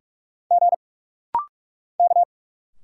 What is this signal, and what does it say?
Morse code, heard as a keyed tone, about 45 wpm, 700 Hz. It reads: G K
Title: Single Letters - Rapid-Fire 45wpm